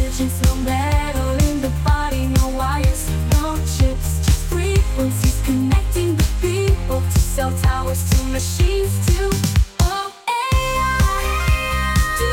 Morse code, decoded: none